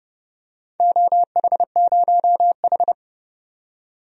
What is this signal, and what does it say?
Morse code, heard as a keyed tone, about 30 wpm, 700 Hz. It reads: OH0H
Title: DXCC-100 Call Signs - 4 Characters - 30wpm